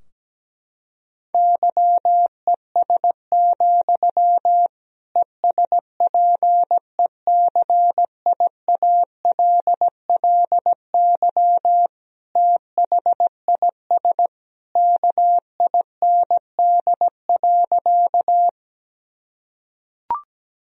Morse code, heard as a keyed tone, about 17 wpm, 700 Hz. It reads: YES, ESPECIALLY THIS KIND.
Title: Sentences from Top 1000 Words - Review Part 4 - 17wpm